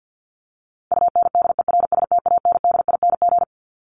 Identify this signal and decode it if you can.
Morse code, heard as a keyed tone, about 50 wpm, 700 Hz. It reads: UNDERSTANDING
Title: ING Suffix - Top 100 Words with Base Words 50wpm